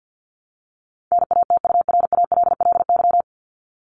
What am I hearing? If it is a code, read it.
Morse code, heard as a keyed tone, about 50 wpm, 700 Hz. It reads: NATURALLY
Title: LY Suffix - Top 100 Words with Base Words 50wpm